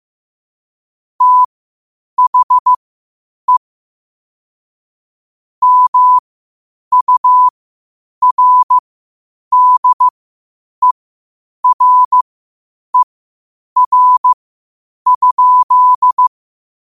Morse code, heard as a keyed tone, about 15 wpm, spaced as for 8 wpm, 1000 Hz. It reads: THE MURDERER?
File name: qotdfiles / 8wpm